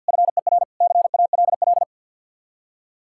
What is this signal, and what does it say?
Morse code, heard as a keyed tone, about 67 wpm, spaced as for 58 wpm, 700 Hz. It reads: FER CALL
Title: Speed-Racing Common QSO Phrases Encore - 50wpm